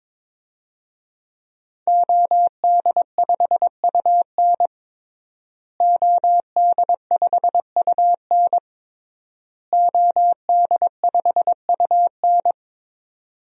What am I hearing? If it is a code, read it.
Morse code, heard as a keyed tone, about 22 wpm, 700 Hz. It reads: OD5UN OD5UN OD5UN
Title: DXCC-100 Call Signs - 5 Characters - Repeated 3x - 22wpm